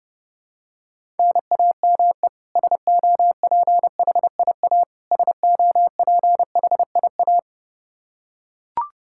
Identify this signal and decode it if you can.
Morse code, heard as a keyed tone, about 30 wpm, 700 Hz. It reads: NAME SOPHIA SOPHIA
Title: Practice QSO - Exchange 1 - Beginning - 30wpm